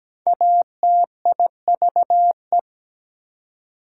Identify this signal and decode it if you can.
Morse code, heard as a keyed tone, about 17 wpm, 700 Hz. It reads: ATIVE
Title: Prefixes and Suffixes 17wpm